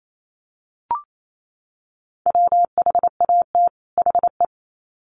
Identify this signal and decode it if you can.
Morse code, heard as a keyed tone, about 28 wpm, 700 Hz. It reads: WHAT HE
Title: Sets of 2 Words 28wpm